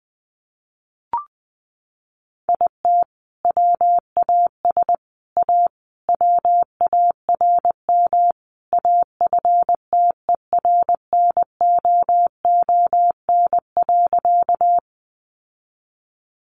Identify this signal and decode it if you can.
Morse code, heard as a keyed tone, about 20 wpm, 700 Hz. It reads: IT WAS A WARM AFTERNOON.
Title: Sentences from Top 1500 Words 20wpm